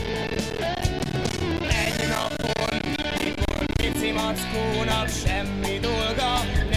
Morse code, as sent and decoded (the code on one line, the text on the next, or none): none